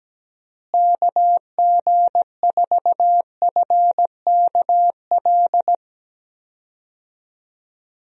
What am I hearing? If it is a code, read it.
Morse code, heard as a keyed tone, about 17 wpm, 700 Hz. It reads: KG4FKL
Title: USA Call Signs - 6 Characters - Repeated 2x - 17wpm